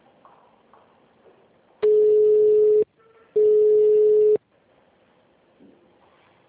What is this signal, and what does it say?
Ringback tone (Europe, 425 Hz).
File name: out- -2003- - - .45